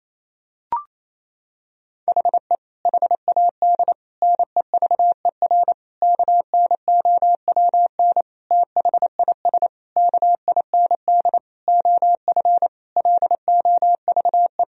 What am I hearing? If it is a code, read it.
Morse code, heard as a keyed tone, about 28 wpm, 700 Hz. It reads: HE HAD NEVER KNOWN THIS KIND OF LOVE
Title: Sentences from Top 1000 Words - Review Part 3 - 28wpm